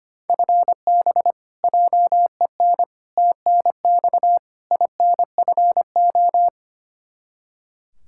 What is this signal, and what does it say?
Morse code, heard as a keyed tone, about 25 wpm, 700 Hz. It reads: FB JEN TNX INFO
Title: Practice QSO - Exchange 3 - More Info - 25wpm